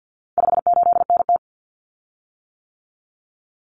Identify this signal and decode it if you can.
Morse code, heard as a keyed tone, about 50 wpm, 700 Hz. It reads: 58NT
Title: Common Cut Numbers - 4-Digit 50wpm